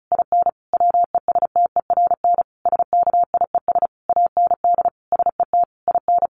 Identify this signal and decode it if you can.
Morse code, heard as a keyed tone, about 34 wpm, 700 Hz. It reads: IN WESTERN SKIES AND SET IN
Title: A Sound Of Thunder 35wpm